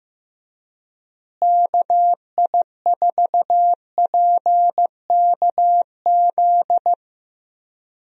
Morse code, heard as a keyed tone, about 15 wpm, 700 Hz. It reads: KI4PKZ
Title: USA Call Signs - 6 Characters - Repeated 3x - 15wpm